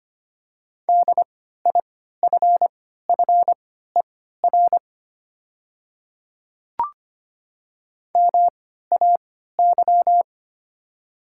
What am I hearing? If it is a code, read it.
Morse code, heard as a keyed tone, about 25 wpm, spaced as for 14 wpm, 700 Hz. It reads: DIFFER MAY
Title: 3x Character Spacing - Top 300 Words - 25wpm